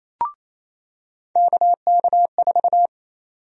Morse code, heard as a keyed tone, about 28 wpm, 700 Hz. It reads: KK4
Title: USA Call Sign Prefixes - 28wpm